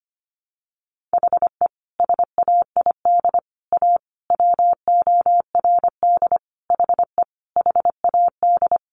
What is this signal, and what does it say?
Morse code, heard as a keyed tone, about 25 wpm, 700 Hz. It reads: HE SAID A WORD HE HAD
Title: Sentences from Top 600 Words 25wpm